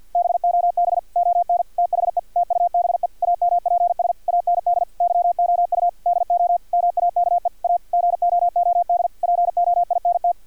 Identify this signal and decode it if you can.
Morse code, heard as a keyed tone, about 50 wpm, 700 Hz, with radio noise. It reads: BOB ON THE TUBE AMPS AND YOU DO MAKE A GOOD POINT
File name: archived / CRSnet